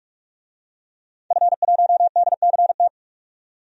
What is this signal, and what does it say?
Morse code, heard as a keyed tone, about 44 wpm, 700 Hz. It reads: F1DCT
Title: DXCC-100 Call Signs - 5 Characters - 45wpm